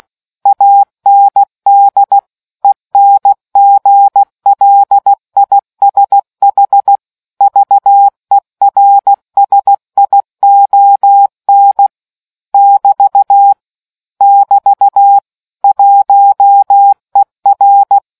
Morse code, heard as a keyed tone, about 16 wpm, 800 Hz. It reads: AND ENGLISH VERSION <BT> <BT> 1ER